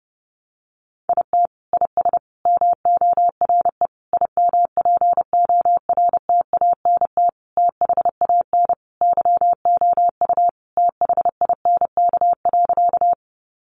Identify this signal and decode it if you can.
Morse code, heard as a keyed tone, about 30 wpm, 700 Hz. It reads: IT IS MORE IMPORTANT THAN YOU THINK.
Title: Sentences from Top 600 Words 30wpm